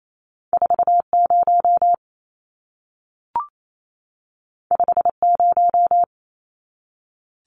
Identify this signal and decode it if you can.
Morse code, heard as a keyed tone, about 28 wpm, 700 Hz. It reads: 40 50
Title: Numbers - 2-Digit 28wpm